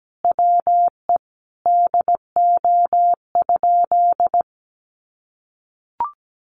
Morse code, heard as a keyed tone, about 17 wpm, 700 Hz. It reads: WE DO?
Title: Sentences from Top 1300 Words 17wpm